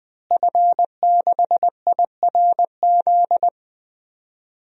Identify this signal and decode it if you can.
Morse code, heard as a keyed tone, about 20 wpm, 700 Hz. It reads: F6IRZ
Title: DXCC-100 Call Signs - 5 Characters - Repeated 2x - 20wpm